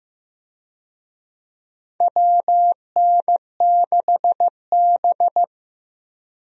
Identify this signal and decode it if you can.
Morse code, heard as a keyed tone, about 15 wpm, 700 Hz. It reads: WN6B
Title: USA Call Signs - 4 Characters - Repeated 2x - 15wpm